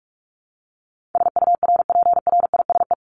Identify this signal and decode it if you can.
Morse code, heard as a keyed tone, about 44 wpm, 700 Hz. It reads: SURPRISE
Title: Top 1751-2000 Words 45wpm